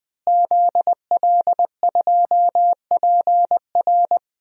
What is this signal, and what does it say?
Morse code, heard as a keyed tone, about 20 wpm, 700 Hz. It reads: ZL2PR
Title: DXCC-100 Call Signs - 5 Characters - Repeated 3x - 20wpm